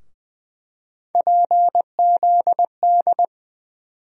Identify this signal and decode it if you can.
Morse code, heard as a keyed tone, about 20 wpm, 700 Hz. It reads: PZD
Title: USA Call Sign Postfixes - 3 Letters 20wpm